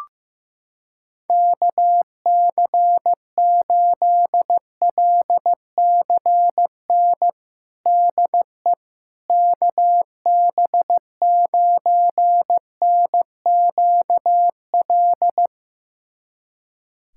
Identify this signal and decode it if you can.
Morse code, heard as a keyed tone, about 15 wpm, 700 Hz. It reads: KC8LCN DE KB9NQL
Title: USA Callsign DE Callsign - 6 Characters - 15wpm